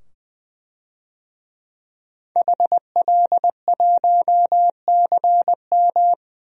Morse code, heard as a keyed tone, about 20 wpm, 700 Hz. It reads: HL1CM
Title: DXCC-100 Call Signs - 5 Characters - 20wpm